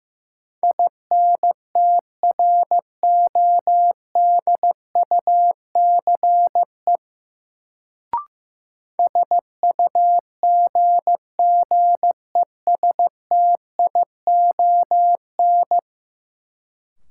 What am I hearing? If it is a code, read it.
Morse code, heard as a keyed tone, about 15 wpm, 700 Hz. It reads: INTRODUCE SUGGESTION
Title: Sentences from Top 1600 Words - New Word Drill - 15wpm